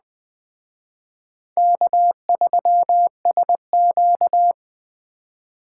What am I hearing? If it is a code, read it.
Morse code, heard as a keyed tone, about 20 wpm, 700 Hz. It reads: K3SQ
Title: Speed Racing - USA 1x2 Call Signs 17wpm